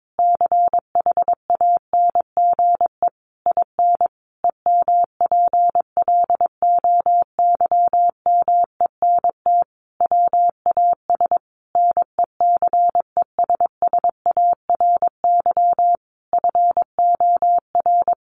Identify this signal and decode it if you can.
Morse code, heard as a keyed tone, about 22 wpm, 700 Hz. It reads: CHANGE IN EMPLOYMENT WAS NECESSARY FOR
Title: Sentences from Top 1500 Words 22wpm